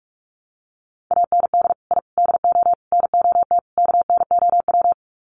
Morse code, heard as a keyed tone, about 44 wpm, 700 Hz. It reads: AND I DO NOT KNOW
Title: Sets of 5 Words 45wpm